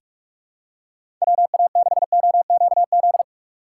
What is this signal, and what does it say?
Morse code, heard as a keyed tone, about 44 wpm, 700 Hz. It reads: WA6OQZ
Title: USA Call Signs - 6 Characters - 6x Character Spacing - 45wpm